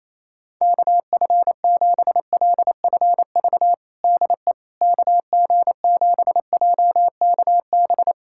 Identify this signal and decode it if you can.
Morse code, heard as a keyed tone, about 28 wpm, 700 Hz. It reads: KF7LFV DE KG7JKB
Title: USA Callsign DE Callsign 28wpm